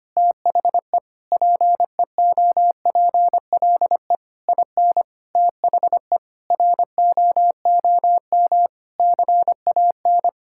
Morse code, heard as a keyed tone, about 25 wpm, 700 Hz. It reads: THE PEOPLE IN THE ROOM CAN
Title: Sentences from Top 1000 Words - Review Part 6 - 25wpm